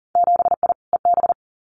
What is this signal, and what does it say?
Morse code, heard as a keyed tone, about 42 wpm, 700 Hz. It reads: 7I EB
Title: Winter Field Day Exchanges Encore 40wpm